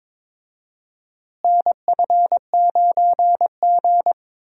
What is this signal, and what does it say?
Morse code, heard as a keyed tone, about 22 wpm, 700 Hz. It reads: NF9G
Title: USA Call Signs - 4 Characters - Repeated 3x - 22wpm